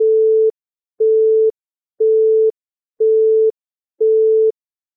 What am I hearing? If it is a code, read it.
Busy tone (France), 440 Hz.
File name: Busy tone (France)